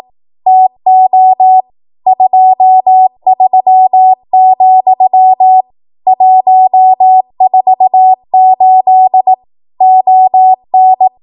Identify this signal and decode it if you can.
Morse code, heard as a keyed tone, about 18 wpm, 750 Hz. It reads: TO 23, 148 ON
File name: Archive / 18 WPM / 18WPM